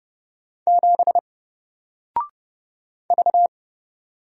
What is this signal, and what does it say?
Morse code, heard as a keyed tone, about 29 wpm, 700 Hz. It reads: 7 V
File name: Single-Letter-Number-30wpm